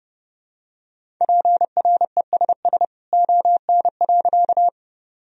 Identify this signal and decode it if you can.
Morse code, heard as a keyed tone, about 30 wpm, 700 Hz. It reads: PRESS ON.
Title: Sentences from Top 1000 Words - Review Part 1 - 30wpm